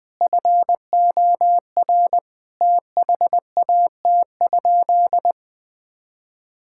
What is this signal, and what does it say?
Morse code, heard as a keyed tone, about 20 wpm, 700 Hz. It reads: FOR THAT?
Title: Sentences from Top 1000 Words - Review Part 6 - 20wpm